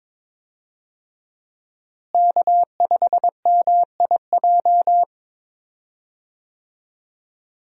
Morse code, heard as a keyed tone, about 22 wpm, 700 Hz. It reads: K5MIJ
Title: USA Call Signs - 5 Characters - Repeated 3x - 22wpm